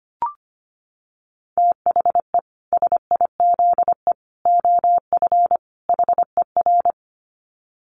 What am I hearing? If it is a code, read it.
Morse code, heard as a keyed tone, about 25 wpm, 700 Hz. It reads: THE SIZE OF HER
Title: Dit Challenge 25wpm